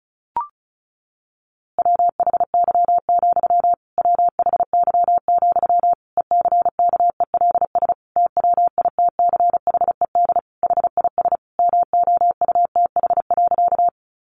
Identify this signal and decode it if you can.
Morse code, heard as a keyed tone, about 34 wpm, 700 Hz. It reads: WHY, WHY, ECKELS TWITCHED HIS MOUTH.
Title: A Sound Of Thunder 35wpm